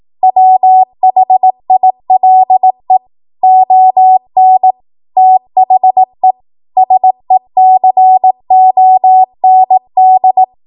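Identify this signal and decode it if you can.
Morse code, heard as a keyed tone, about 18 wpm, 750 Hz. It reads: WHILE ON THE SECOND
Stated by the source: ARRL 18 wpm Morse code practice archive transmission.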